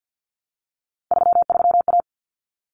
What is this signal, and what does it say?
Morse code, heard as a keyed tone, about 50 wpm, 700 Hz. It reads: 33A
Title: Common Cut Numbers - 3-Digit 50wpm